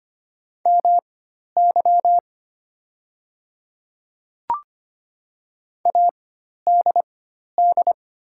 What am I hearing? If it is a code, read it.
Morse code, heard as a keyed tone, about 25 wpm, spaced as for 12 wpm, 700 Hz. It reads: MY ADD
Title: 4x Character Spacing - Top 300 Words - 25wpm